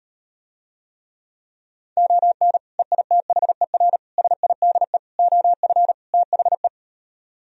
Morse code, heard as a keyed tone, about 38 wpm, 700 Hz. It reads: ON EITHER SIDE OF THE